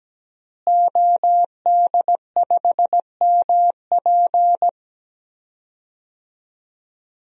Morse code, heard as a keyed tone, about 17 wpm, 700 Hz. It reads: OD5MP
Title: DXCC-100 Call Signs - 5 Characters - Repeated 2x - 17wpm